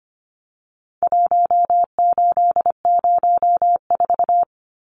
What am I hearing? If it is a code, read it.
Morse code, heard as a keyed tone, about 25 wpm, 700 Hz. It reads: 1804
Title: Numbers - Contest Style - 4-Digit 25wpm